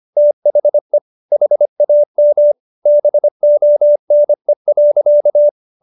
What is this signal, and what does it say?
Morse code, heard as a keyed tone, about 25 wpm, 590 Hz. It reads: THE HAM BONE.